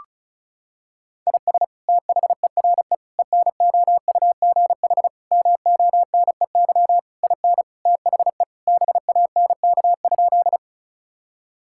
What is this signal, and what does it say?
Morse code, heard as a keyed tone, about 34 wpm, 700 Hz. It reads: IS THERE ENOUGH MONEY IN THE BANK?
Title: Sentences from Top 1000 Words - Review Part 6 - 35wpm